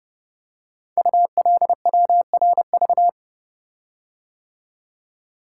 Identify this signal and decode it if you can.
Morse code, heard as a keyed tone, about 28 wpm, 700 Hz. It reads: ULWRV
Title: Code Groups - 5 Letters - 30wpm